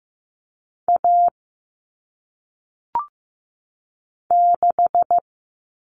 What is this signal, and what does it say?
Morse code, heard as a keyed tone, about 15 wpm, 700 Hz. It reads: A 6
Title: All Cut Numbers - 1-Digit 15wpm